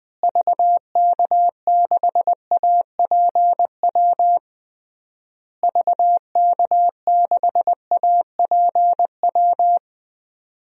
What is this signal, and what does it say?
Morse code, heard as a keyed tone, about 20 wpm, 700 Hz. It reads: VK6APW VK6APW
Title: DXCC-100 Call Signs - 6 Characters - Repeated 3x - 20wpm